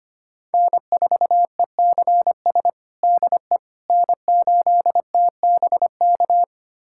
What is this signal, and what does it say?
Morse code, heard as a keyed tone, about 25 wpm, 700 Hz. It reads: N4ECS DE N8TBK
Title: USA Callsign DE Callsign - 5 Characters - 25wpm